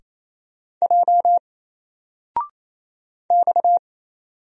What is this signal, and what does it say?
Morse code, heard as a keyed tone, about 29 wpm, 700 Hz. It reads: J X
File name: Single-Letter-Number-28wpm